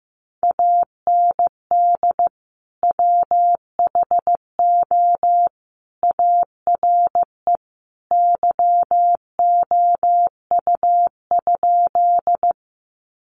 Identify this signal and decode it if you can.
Morse code, heard as a keyed tone, about 15 wpm, 700 Hz. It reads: AND WHO ARE YOU?
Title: Sentences from Top 100 Words 15wpm